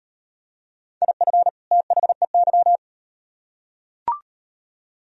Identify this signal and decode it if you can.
Morse code, heard as a keyed tone, about 38 wpm, 700 Hz. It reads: IF THEY